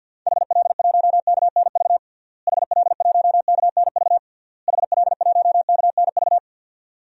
Morse code, heard as a keyed tone, about 46 wpm, 700 Hz. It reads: HL1KNV HL1KNV HL1KNV
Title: DXCC-100 Call Signs - 6 Characters - Repeated 3x - 50wpm